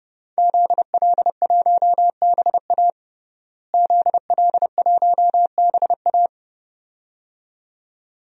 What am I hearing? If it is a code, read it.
Morse code, heard as a keyed tone, about 30 wpm, 700 Hz. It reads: ZL1BA ZL1BA
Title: DXCC-100 Call Signs - 5 Characters - Repeated 2x - 30wpm